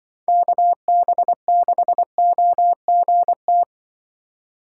Morse code, heard as a keyed tone, about 24 wpm, 700 Hz. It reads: KB6OGT
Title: Speed Racing - USA 2x3 Call Signs 20wpm